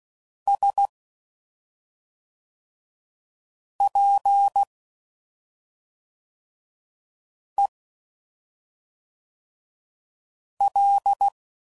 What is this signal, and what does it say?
Morse code, heard as a keyed tone, about 16 wpm, 800 Hz. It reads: S P E L
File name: Suffixes / Special ING's / SlowToFaster ING's